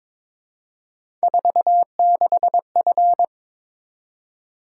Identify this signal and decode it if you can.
Morse code, heard as a keyed tone, about 22 wpm, 700 Hz. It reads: 46F
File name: practice / 22wpm / Three Letter-Number Combinations - 22wpm